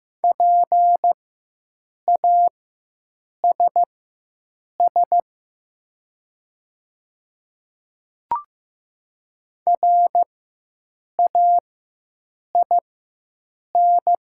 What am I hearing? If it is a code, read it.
Morse code, heard as a keyed tone, about 15 wpm, spaced as for 7 wpm, 700 Hz. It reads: PASS RAIN
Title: 4x Character Spacing - Top 500 Words - 15wpm